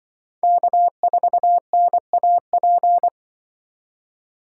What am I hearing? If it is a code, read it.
Morse code, heard as a keyed tone, about 23 wpm, 700 Hz. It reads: K4NAP
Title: Speed Racing - USA 1x3 Call Signs 20wpm